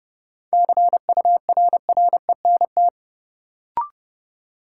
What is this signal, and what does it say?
Morse code, heard as a keyed tone, about 30 wpm, 700 Hz. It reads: CURRENT
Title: Sentences from Top 1000 Words - Review Part 4 - New Word Drill - 30wpm